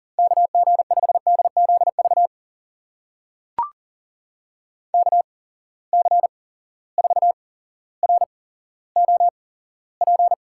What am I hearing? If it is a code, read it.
Morse code, heard as a keyed tone, about 40 wpm, spaced as for 17 wpm, 700 Hz. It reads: KG5DZV KC4ROP
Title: USA Call Signs - 6 Characters - 8x Character Spacing - 40wpm